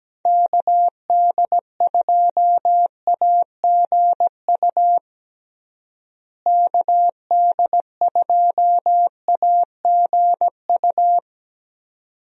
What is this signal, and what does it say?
Morse code, heard as a keyed tone, about 17 wpm, 700 Hz. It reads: KD2AGU KD2AGU
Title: USA Call Signs - 6 Characters - Repeated 2x - 17wpm